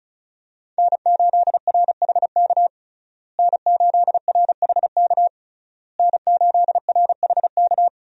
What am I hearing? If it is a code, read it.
Morse code, heard as a keyed tone, about 34 wpm, 700 Hz. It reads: N8RHK N8RHK N8RHK
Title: USA Call Signs - 5 Characters - Repeated 3x - 35wpm